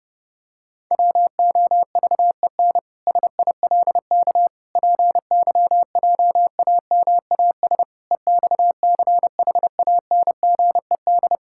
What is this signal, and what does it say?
Morse code, heard as a keyed tone, about 30 wpm, 700 Hz. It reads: WOVEN SILK PYJAMAS EXCHANGED
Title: Pangram Challenge v2 - 30wpm